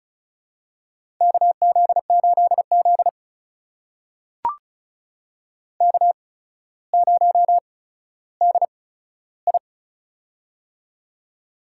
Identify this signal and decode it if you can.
Morse code, heard as a keyed tone, about 36 wpm, spaced as for 15 wpm, 700 Hz. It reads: KZ8Z K0DI
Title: USA Call Signs - 4 Characters - 8x Character Spacing - 35wpm